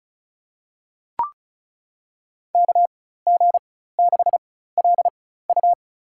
Morse code, heard as a keyed tone, about 34 wpm, spaced as for 16 wpm, 690 Hz. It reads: KG6LU
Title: USA Call Signs - 4x Character Spacing - 35wpm